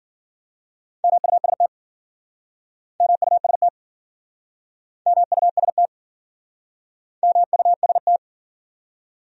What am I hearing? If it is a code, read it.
Morse code, heard as a keyed tone, about 57 wpm, 700 Hz. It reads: MUST MUST MUST MUST